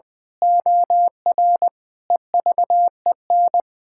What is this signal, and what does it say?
Morse code, heard as a keyed tone, about 21 wpm, 700 Hz. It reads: OR EVEN
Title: Pangram Challenge v2 - 20wpm